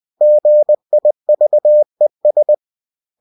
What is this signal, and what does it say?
Morse code, heard as a keyed tone, about 20 wpm, 600 Hz. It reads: GIVES
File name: cw-books / winnie-the-pooh / winnie-20wpm-page-0101